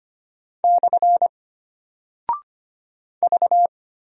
Morse code, heard as a keyed tone, about 25 wpm, 700 Hz. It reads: / V
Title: Single Character (Letter:Number:Punctuation:Prosign) Alt 25wpm